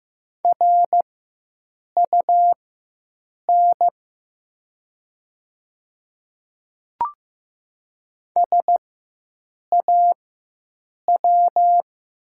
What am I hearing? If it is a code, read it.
Morse code, heard as a keyed tone, about 15 wpm, spaced as for 7 wpm, 700 Hz. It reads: RUN SAW
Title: 4x Character Spacing - Top 300 Words - 15wpm